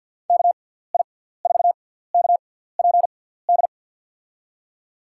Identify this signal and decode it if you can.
Morse code, heard as a keyed tone, about 53 wpm, spaced as for 17 wpm, 700 Hz. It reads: KI4KPD
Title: USA Call Signs - 6 Characters - 6x Character Spacing - 50wpm